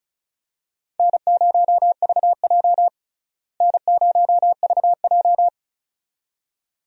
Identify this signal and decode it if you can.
Morse code, heard as a keyed tone, about 36 wpm, 700 Hz. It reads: N0VJ N0VJ
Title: USA Call Signs - 4 Characters - Repeated 2x - 35wpm